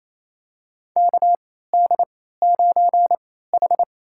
Morse code, heard as a keyed tone, about 28 wpm, spaced as for 16 wpm, 700 Hz. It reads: KD9H
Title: USA Call Signs - 4 Characters - 3x Character Spacing - 28wpm